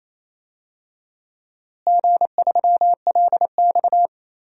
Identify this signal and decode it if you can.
Morse code, heard as a keyed tone, about 28 wpm, 700 Hz. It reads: G3LX
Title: DXCC-100 Call Signs - 4 Characters - Repeated 2x - 28wpm